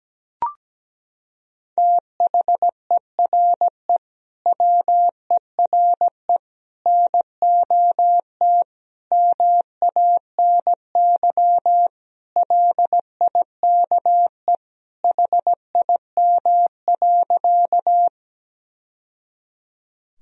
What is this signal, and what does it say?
Morse code, heard as a keyed tone, about 17 wpm, 700 Hz. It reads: THERE WERE NOT MANY LIKE HIM.
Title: Sentences from Top 1000 Words - Review Part 1 - 17wpm